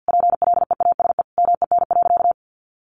Morse code, heard as a keyed tone, about 50 wpm, 700 Hz. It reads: PLEASE MEN.
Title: Sentences from Top 1500 Words 50wpm